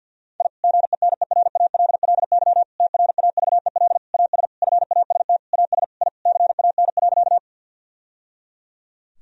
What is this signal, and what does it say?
Morse code, heard as a keyed tone, about 50 wpm, 700 Hz. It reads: I GENERALLY TRAVEL AS FAST AS I CAN.
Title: Sentences from Top 1000 Words - Review Part 4 - 50wpm